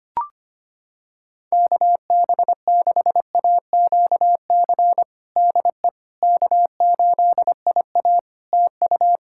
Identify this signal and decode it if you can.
Morse code, heard as a keyed tone, about 25 wpm, 700 Hz. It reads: KB6AQC DE K8IA TU
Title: K1USN SST QSO Practice 25wpm